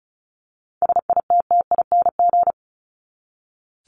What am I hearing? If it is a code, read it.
Morse code, heard as a keyed tone, about 34 wpm, 700 Hz. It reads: SITTING
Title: ING Suffix - Top 100 Words with Base Words 35wpm